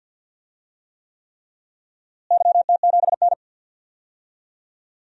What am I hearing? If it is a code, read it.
Morse code, heard as a keyed tone, about 52 wpm, 700 Hz. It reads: YT7N
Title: DXCC-100 Call Signs - 4 Characters - Repeated 3x - 50wpm